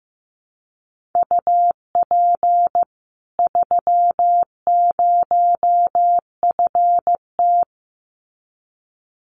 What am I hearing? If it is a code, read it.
Morse code, heard as a keyed tone, about 15 wpm, 700 Hz. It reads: UP 30FT
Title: Common QSO Phrases 15wpm